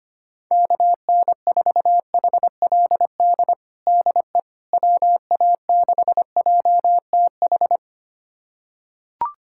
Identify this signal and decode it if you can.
Morse code, heard as a keyed tone, about 25 wpm, 700 Hz. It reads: KN4HLD DE WA6JTH
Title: USA Callsign DE Callsign 25wpm